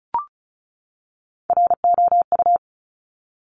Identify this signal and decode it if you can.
Morse code, heard as a keyed tone, about 36 wpm, 700 Hz. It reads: ROU
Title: Top Three Letter Combinations in Words 35wpm